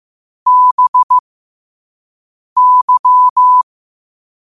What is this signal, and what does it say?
Morse code, heard as a keyed tone, about 15 wpm, spaced as for 5 wpm, 1000 Hz. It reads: BY
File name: qotdfiles / 5wpm